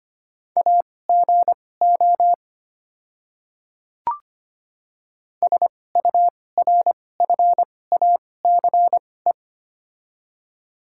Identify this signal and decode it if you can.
Morse code, heard as a keyed tone, about 25 wpm, spaced as for 18 wpm, 700 Hz. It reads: AGO SURFACE
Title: 2x Character Spacing - Top 500 Words - 25wpm